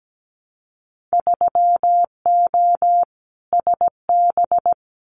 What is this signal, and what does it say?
Morse code, heard as a keyed tone, about 17 wpm, 700 Hz. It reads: 3O SB
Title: Winter Field Day Exchanges Encore 17wpm